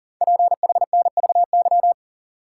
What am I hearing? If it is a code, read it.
Morse code, heard as a keyed tone, about 44 wpm, 700 Hz. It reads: PHNVY
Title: Code Groups - 5 Letters - 40wpm